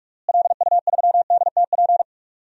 Morse code, heard as a keyed tone, about 46 wpm, 700 Hz. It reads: LU3DTP
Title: DXCC-100 Call Signs - 6 Characters - Repeated 3x - 45wpm